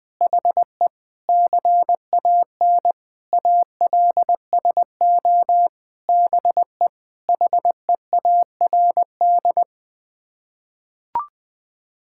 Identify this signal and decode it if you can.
Morse code, heard as a keyed tone, about 20 wpm, 700 Hz. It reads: HE CAN ALSO BE HEARD
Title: Sets of 9 Words - 20wpm